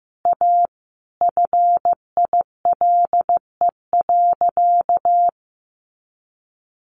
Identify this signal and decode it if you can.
Morse code, heard as a keyed tone, about 14 wpm, 700 Hz. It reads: A FILE.
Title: Sentences from Top 1200 Words 15wpm